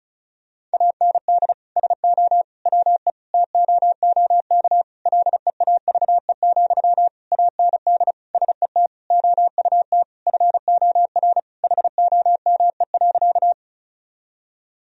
Morse code, heard as a keyed tone, about 36 wpm, 700 Hz. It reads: AND SO WE TOOK LEAVE, AND SET OUT FOR HOME.
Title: Sentences from Top 1000 Words - Review Part 1 - 35wpm